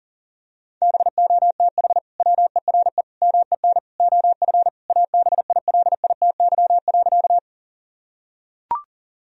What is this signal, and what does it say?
Morse code, heard as a keyed tone, about 37 wpm, 700 Hz. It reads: BOTH WERE MEN OF ABILITY.
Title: Sentences from Top 1000 Words - Review Part 3 - 40wpm